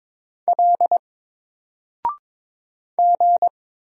Morse code, heard as a keyed tone, about 22 wpm, 700 Hz. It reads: L G
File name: Single-Letter-Number-22wpm